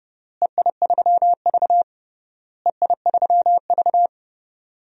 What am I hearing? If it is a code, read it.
Morse code, heard as a keyed tone, about 32 wpm, 700 Hz. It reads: EI3V EI3V
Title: DXCC-100 Call Signs - 4 Characters - Repeated 3x - 30wpm